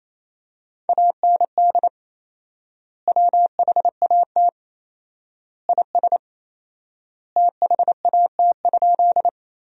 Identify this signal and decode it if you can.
Morse code, heard as a keyed tone, about 28 wpm, 700 Hz. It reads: AND WHAT IS THAT?
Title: 4x Word-Spacing - Sentences from Top 100 Words - 28wpm